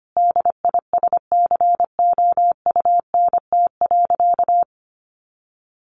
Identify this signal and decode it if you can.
Morse code, heard as a keyed tone, about 25 wpm, 700 Hz. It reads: DISCOUNT.
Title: Sentences from Top 1200 Words 25wpm